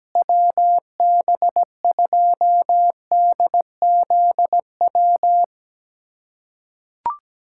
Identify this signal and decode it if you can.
Morse code, heard as a keyed tone, about 17 wpm, 700 Hz. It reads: WB2DZW
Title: USA Callsign DE Callsign - 6 Characters - 17wpm